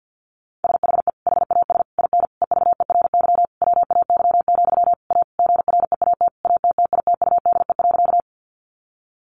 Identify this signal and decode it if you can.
Morse code, heard as a keyed tone, about 50 wpm, 700 Hz. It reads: SHE HAS IN EVERY WAY, A GREAT ATTITUDE.
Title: Sentences from Top 1300 Words 50wpm